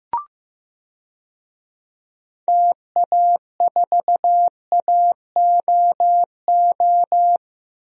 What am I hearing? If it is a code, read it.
Morse code, heard as a keyed tone, about 15 wpm, 700 Hz. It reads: TA4AOO
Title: DXCC-100 Call Signs - 6 Characters - Repeated 3x - 15wpm